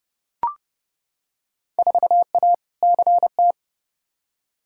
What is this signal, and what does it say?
Morse code, heard as a keyed tone, about 31 wpm, 700 Hz. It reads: 4A CT
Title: Field Day Exchanges Encore - 30wpm